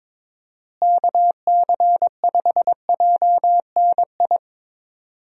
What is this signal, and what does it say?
Morse code, heard as a keyed tone, about 22 wpm, 700 Hz. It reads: KC5JNI
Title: USA Call Signs - 6 Characters - Repeated 3x - 22wpm